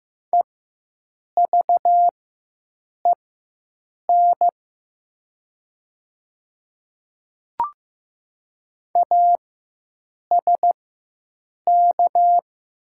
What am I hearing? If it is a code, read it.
Morse code, heard as a keyed tone, about 15 wpm, spaced as for 7 wpm, 700 Hz. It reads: EVEN ASK
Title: 4x Character Spacing - Top 300 Words - 15wpm